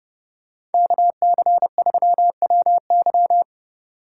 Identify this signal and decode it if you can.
Morse code, heard as a keyed tone, about 29 wpm, 700 Hz. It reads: KC3WY